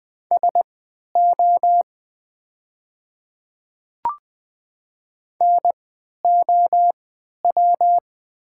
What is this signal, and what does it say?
Morse code, heard as a keyed tone, about 20 wpm, spaced as for 11 wpm, 690 Hz. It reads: SO NOW
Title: 3x Character Spacing - Top 100 Words - 20wpm